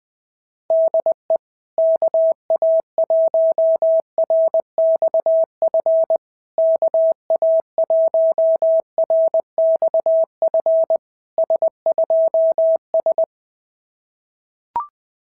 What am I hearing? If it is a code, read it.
Morse code, heard as a keyed tone, about 20 wpm, 650 Hz. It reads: DE KA1RXF KA1RXF S2S
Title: SOTA QSO Practice 20wpm